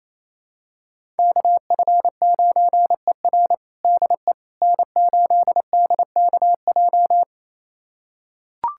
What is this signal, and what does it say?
Morse code, heard as a keyed tone, about 28 wpm, 700 Hz. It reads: KF9ER DE N8DKJ
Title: USA Callsign DE Callsign 28wpm